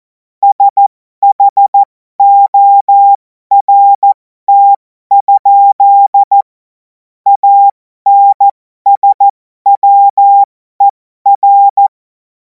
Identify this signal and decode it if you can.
Morse code, heard as a keyed tone, about 14 wpm, spaced as for 12 wpm, 800 Hz. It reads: SHORT? ANSWER
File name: Morsekurs / Witze